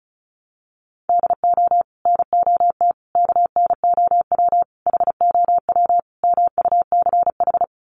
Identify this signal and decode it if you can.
Morse code, heard as a keyed tone, about 34 wpm, 700 Hz. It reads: DO NOT KNOW HOW MUCH
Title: Sets of 5 Words Encore 35wpm